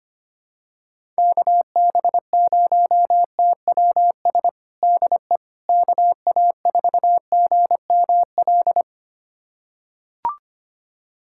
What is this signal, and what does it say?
Morse code, heard as a keyed tone, about 25 wpm, 700 Hz. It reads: KB0TWS DE KA4GML
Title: USA Callsign DE Callsign - 6 Characters - 25wpm